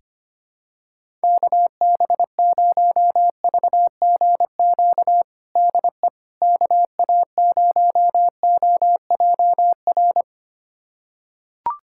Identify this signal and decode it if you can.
Morse code, heard as a keyed tone, about 25 wpm, 700 Hz. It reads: KB0VGQ DE KA0OJR
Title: USA Callsign DE Callsign 25wpm